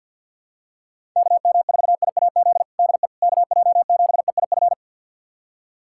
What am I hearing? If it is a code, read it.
Morse code, heard as a keyed tone, about 50 wpm, 700 Hz. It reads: KM4IAZ DE KJ7EIF
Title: USA Callsign DE Callsign 50wpm